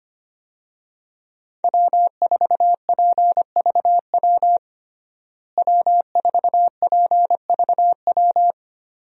Morse code, heard as a keyed tone, about 25 wpm, 700 Hz. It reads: W4PVW W4PVW
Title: USA Call Signs - 5 Characters - Repeated 3x - 25wpm